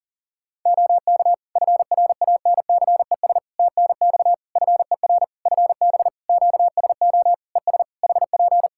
Morse code, heard as a keyed tone, about 40 wpm, 700 Hz. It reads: OK FRANCES TNX FER FB QSO ES HP
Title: Practice QSO - Exchange 4 - Ending - 40wpm